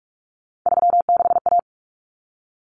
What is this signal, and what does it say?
Morse code, heard as a keyed tone, about 44 wpm, 700 Hz. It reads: 36A
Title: Common Cut Numbers - 3-Digit 45wpm